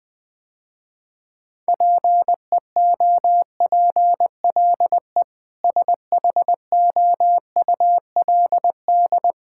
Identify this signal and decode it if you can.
Morse code, heard as a keyed tone, about 20 wpm, 700 Hz. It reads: PEOPLE SHOULD
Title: Sentences with 8-Word Phrases - Spoken Beginning - 20wpm